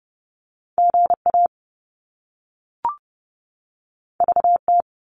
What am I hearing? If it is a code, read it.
Morse code, heard as a keyed tone, about 30 wpm, 700 Hz. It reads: GA VT
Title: US States 30wpm